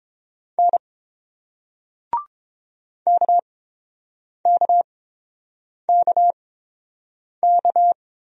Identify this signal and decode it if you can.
Morse code, heard as a keyed tone, about 27 wpm, spaced as for 7 wpm, 700 Hz. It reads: N KKKK
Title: Speed-Racing Single Letters 22wpm